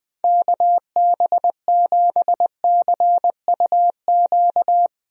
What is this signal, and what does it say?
Morse code, heard as a keyed tone, about 20 wpm, 700 Hz. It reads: KB7CUQ
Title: USA Call Signs - 6 Characters - Repeated 3x - 20wpm